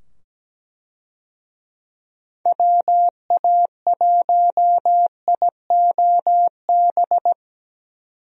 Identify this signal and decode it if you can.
Morse code, heard as a keyed tone, about 17 wpm, 700 Hz. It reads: WA1IOB
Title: USA Call Signs - 6 Characters - 17wpm